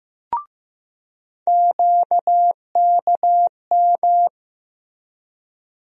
Morse code, heard as a keyed tone, about 15 wpm, 700 Hz. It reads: QKM
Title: USA Call Sign Postfixes - 3 Letters 15wpm